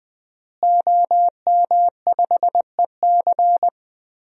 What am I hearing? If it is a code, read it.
Morse code, heard as a keyed tone, about 20 wpm, 700 Hz. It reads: OM5EC
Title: DXCC-100 Call Signs - 5 Characters - Repeated 3x - 20wpm